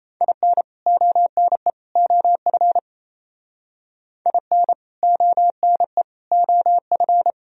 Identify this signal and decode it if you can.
Morse code, heard as a keyed tone, about 36 wpm, 700 Hz. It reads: IN ONE OF IN ONE OF